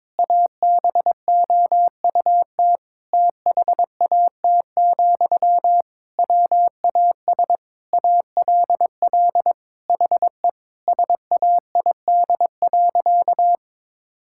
Morse code, heard as a keyed tone, about 22 wpm, 700 Hz. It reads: ABOUT THAT, WAS ALL HE SAID.
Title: Sentences from Top 1600 Words 22wpm